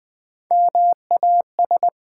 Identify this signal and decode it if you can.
Morse code, heard as a keyed tone, about 21 wpm, 700 Hz. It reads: MAS
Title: Build A Word - 1 Letter At A Time - 4-Letter Words - 20wpm